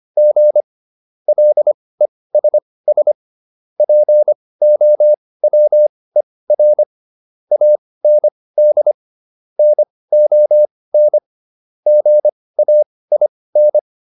Morse code, heard as a keyed tone, about 25 wpm, spaced as for 18 wpm, 600 Hz. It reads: G LESS POWER AND NON GAIN